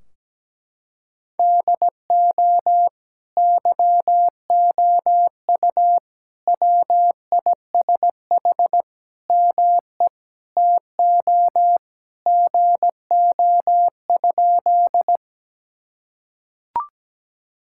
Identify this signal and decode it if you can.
Morse code, heard as a keyed tone, about 17 wpm, 700 Hz. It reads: DO YOU WISH ME TO GO?
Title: Sentences from Top 1600 Words 17wpm